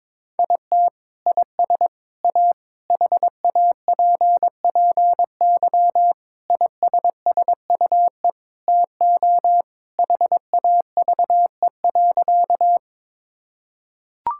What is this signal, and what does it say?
Morse code, heard as a keyed tone, about 22 wpm, 700 Hz. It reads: IT IS A HAPPY ISSUE TO HAVE.
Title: Sentences from Top 1000 Words - Review Part 6 - 22wpm